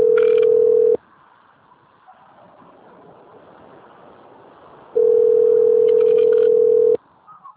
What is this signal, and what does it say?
Ringback tone (North America), 440 + 480 Hz.
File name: out- - - - - .7067